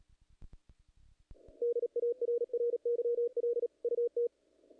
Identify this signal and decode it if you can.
Morse code, heard as a keyed tone, about 33 wpm, 470 Hz. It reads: DARRYL UT
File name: contests / naqp-cw-k5tr-op-k5pi- -1800